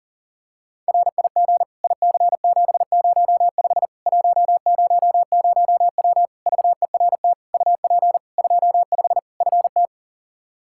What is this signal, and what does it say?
Morse code, heard as a keyed tone, about 40 wpm, 700 Hz. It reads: RIG IC705 100W VERT UP 25 FT
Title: K1USN SST QSO Practice 40wpm